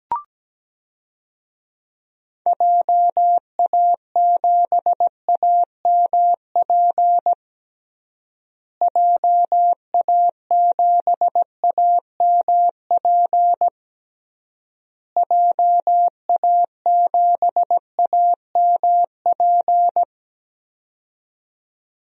Morse code, heard as a keyed tone, about 17 wpm, 700 Hz. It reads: JA7AMP JA7AMP JA7AMP
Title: DXCC-100 Call Signs - 6 Characters - Repeated 3x - 17wpm